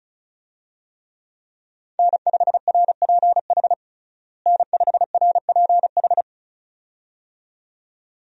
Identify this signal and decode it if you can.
Morse code, heard as a keyed tone, about 36 wpm, 700 Hz. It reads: N5RPH N5RPH
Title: USA Call Signs - 5 Characters - Repeated 2x - 35wpm